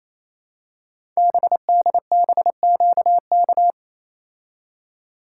Morse code, heard as a keyed tone, about 28 wpm, 700 Hz. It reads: BDBQK
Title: Code Groups - 5 Letters - 28wpm